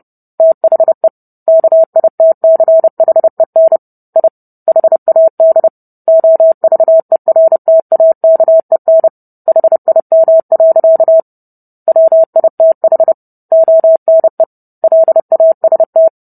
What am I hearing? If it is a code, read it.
Morse code, heard as a keyed tone, about 30 wpm, 650 Hz. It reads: THE KITCHEN I HAD OVERTAKEN HIM. WITH ONE LAST